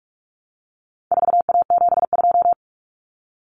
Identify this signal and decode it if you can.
Morse code, heard as a keyed tone, about 44 wpm, 700 Hz. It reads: 4A72
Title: Common Cut Numbers - 4-Digit 45wpm